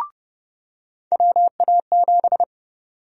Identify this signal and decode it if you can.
Morse code, heard as a keyed tone, about 31 wpm, 700 Hz. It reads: WA7
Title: USA Call Sign Prefixes - 30wpm